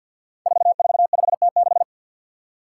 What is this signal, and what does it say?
Morse code, heard as a keyed tone, about 51 wpm, 700 Hz. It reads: 445T6
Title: Five Challenging Letter-Number Combinations 50wpm